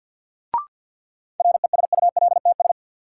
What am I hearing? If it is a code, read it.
Morse code, heard as a keyed tone, about 52 wpm, 700 Hz. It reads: RESULTS
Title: Sets of 4 Words - New Word Drill - 50wpm